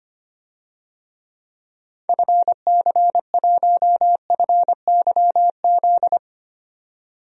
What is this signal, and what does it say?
Morse code, heard as a keyed tone, about 25 wpm, 700 Hz. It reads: FC1FYZ
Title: DXCC-100 Call Signs - 6 Characters - 25wpm